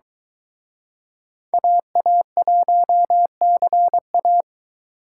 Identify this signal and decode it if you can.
Morse code, heard as a keyed tone, about 23 wpm, 700 Hz. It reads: AA1CA